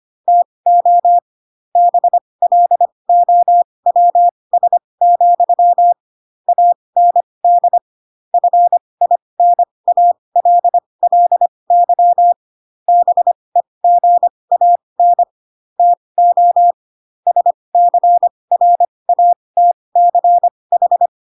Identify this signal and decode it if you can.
Morse code, heard as a keyed tone, about 25 wpm, spaced as for 20 wpm, 700 Hz. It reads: TO BLOWS, AND FINALLY BEGAN TO SCRATCH
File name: Pinocchio / Pinocchio ch2 15 25 ik1hge